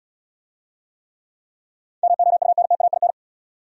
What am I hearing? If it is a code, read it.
Morse code, heard as a keyed tone, about 64 wpm, 700 Hz. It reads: MOUNTAIN